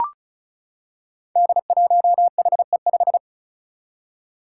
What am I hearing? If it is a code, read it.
Morse code, heard as a keyed tone, about 36 wpm, 700 Hz. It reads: D1HE5
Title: Five Challenging Letter-Number Combinations 35wpm